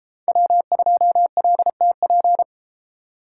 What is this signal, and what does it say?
Morse code, heard as a keyed tone, about 29 wpm, 700 Hz. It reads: W2LTP